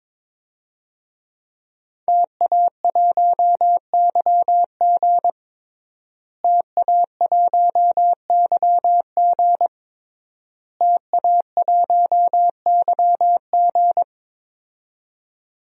Morse code, heard as a keyed tone, about 22 wpm, 700 Hz. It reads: TA1YG TA1YG TA1YG
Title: DXCC-100 Call Signs - 5 Characters - Repeated 3x - 22wpm